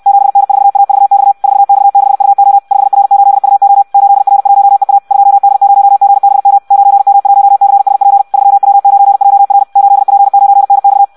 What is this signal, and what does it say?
Morse code, heard as a keyed tone, about 64 wpm, 800 Hz. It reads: 7N4AVX 4X6UO 5R8UO 8R1EA 2D0GBM 9M1CSP 3W9JR 8P9NX
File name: Audio practica CW / qrq-tx-66 2